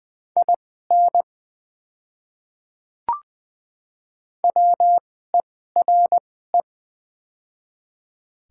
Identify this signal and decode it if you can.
Morse code, heard as a keyed tone, about 20 wpm, spaced as for 15 wpm, 700 Hz. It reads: IN WERE
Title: 2x Character Spacing - Top 100 Words - 20wpm